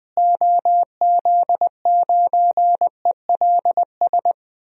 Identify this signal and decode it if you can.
Morse code, heard as a keyed tone, about 20 wpm, 700 Hz. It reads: OZ9ELS
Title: DXCC-100 Call Signs - 6 Characters - Repeated 2x - 20wpm